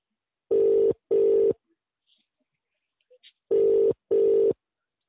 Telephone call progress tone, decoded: ringback tone